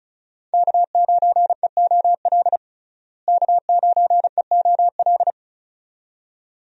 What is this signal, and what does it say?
Morse code, heard as a keyed tone, about 36 wpm, 700 Hz. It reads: K9EOL K9EOL
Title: USA Call Signs - 5 Characters - Repeated 2x - 35wpm